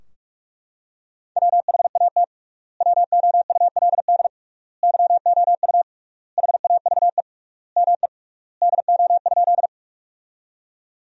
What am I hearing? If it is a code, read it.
Morse code, heard as a keyed tone, about 42 wpm, 700 Hz. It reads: WHAT WOULD YOU HAVE ME DO?
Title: 3x Word-Spacing - Sentences from Top 200 Words - 45wpm